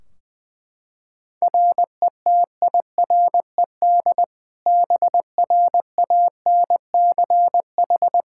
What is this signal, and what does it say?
Morse code, heard as a keyed tone, about 20 wpm, 700 Hz. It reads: RETIRED BRANCH
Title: Practice QSO - Exchange 3 - More Info - 20wpm